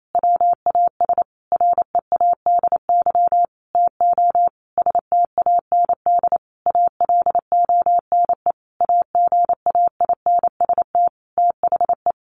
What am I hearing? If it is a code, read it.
Morse code, heard as a keyed tone, about 28 wpm, 700 Hz. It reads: WAS READY TO STAND ALONE AGAINST THE
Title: Sentences from Top 1400 Words 28wpm